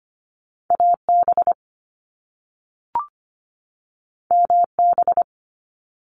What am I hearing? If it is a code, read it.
Morse code, heard as a keyed tone, about 25 wpm, 700 Hz. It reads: AB MB
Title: Canadian Provinces and Territories 25wpm